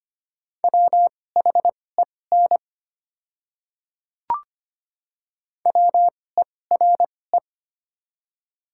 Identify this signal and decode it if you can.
Morse code, heard as a keyed tone, about 25 wpm, spaced as for 18 wpm, 700 Hz. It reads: WHEN WERE
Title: 2x Character Spacing - Top 200 Words - 25wpm